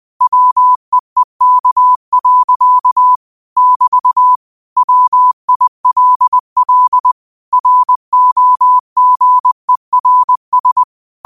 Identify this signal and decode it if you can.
Morse code, heard as a keyed tone, about 20 wpm, 1000 Hz. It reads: WEEK. <BT> WILL ROGERS